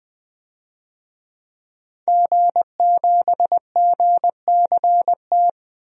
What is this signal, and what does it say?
Morse code, heard as a keyed tone, about 20 wpm, 700 Hz. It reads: G7GCT
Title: DXCC-100 Call Signs - 5 Characters - 20wpm